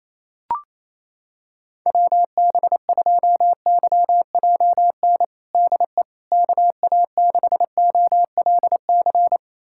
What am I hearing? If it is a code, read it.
Morse code, heard as a keyed tone, about 28 wpm, 700 Hz. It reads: WB2YJN DE KA6OLC
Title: USA Callsign DE Callsign - 6 Characters - 28wpm